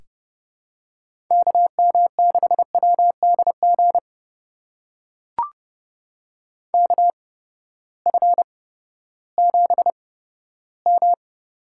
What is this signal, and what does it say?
Morse code, heard as a keyed tone, about 30 wpm, spaced as for 15 wpm, 700 Hz. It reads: KM6WDG KF7M
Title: USA Call Signs - 6 Characters - 8x Character Spacing - 30wpm